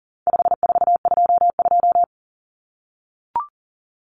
Morse code, heard as a keyed tone, about 40 wpm, 700 Hz. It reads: 5422
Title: Numbers - 4-Digit 40wpm